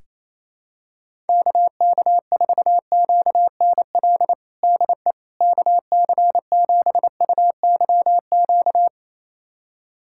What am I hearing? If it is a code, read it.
Morse code, heard as a keyed tone, about 27 wpm, 700 Hz. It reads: KK4QNL DE KC7UYQ
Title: USA Callsign DE Callsign - 6 Characters - 28wpm